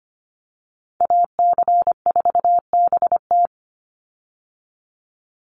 Morse code, heard as a keyed tone, about 25 wpm, 700 Hz. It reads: AC4BT
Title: Call Signs - 25wpm